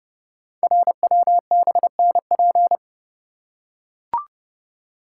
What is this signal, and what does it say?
Morse code, heard as a keyed tone, about 30 wpm, 700 Hz. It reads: RWBNP
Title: Code Groups - 5 Letters - 30wpm